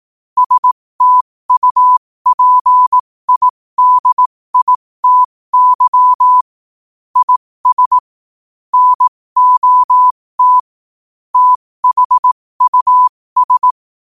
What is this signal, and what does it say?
Morse code, heard as a keyed tone, about 18 wpm, spaced as for 15 wpm, 1000 Hz. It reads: STUPIDITY IS NOT THUS
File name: qotdfiles / 15wpm